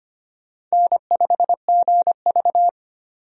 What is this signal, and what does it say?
Morse code, heard as a keyed tone, about 25 wpm, 700 Hz. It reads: N5GV
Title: USA Call Signs - 4 Characters - 25wpm